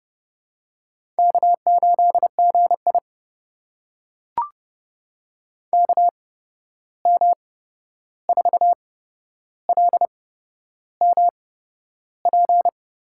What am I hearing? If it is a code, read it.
Morse code, heard as a keyed tone, about 30 wpm, spaced as for 11 wpm, 700 Hz. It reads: K8GI KM4LMP
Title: USA Call Signs - 8x Character Spacing - 30wpm